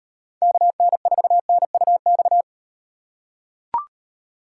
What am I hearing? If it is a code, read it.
Morse code, heard as a keyed tone, about 35 wpm, 700 Hz. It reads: KN4NUX